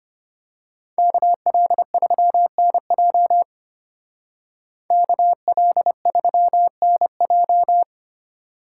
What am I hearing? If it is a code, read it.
Morse code, heard as a keyed tone, about 27 wpm, 700 Hz. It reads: KL3NJ KL3NJ
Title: Speed Racing - USA 2x2 Call Signs 25wpm